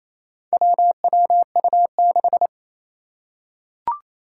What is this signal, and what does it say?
Morse code, heard as a keyed tone, about 28 wpm, 700 Hz. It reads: WWU6
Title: Four Challenging Letter-Number Combinations 28wpm